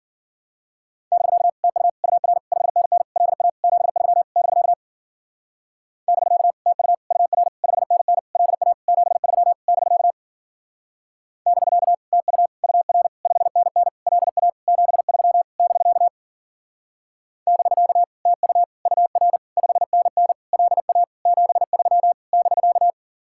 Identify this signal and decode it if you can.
Morse code, heard as a keyed tone, about 49 wpm, 700 Hz. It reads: <BK> TU UR 5NN LA 73 <BK> <BK> TU UR 5NN LA 73 <BK> <BK> TU UR 5NN LA 73 <BK> <BK> TU UR 5NN LA 73 <BK>
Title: Speed-Racing Common QSO Phrases - 40wpm